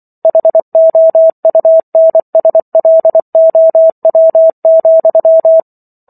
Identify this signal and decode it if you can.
Morse code, heard as a keyed tone, about 24 wpm, 650 Hz. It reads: HOUNSLOW,